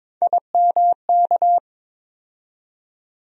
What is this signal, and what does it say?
Morse code, heard as a keyed tone, about 22 wpm, 700 Hz. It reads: IMK
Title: USA Call Sign Postfixes - 3 Letters 22wpm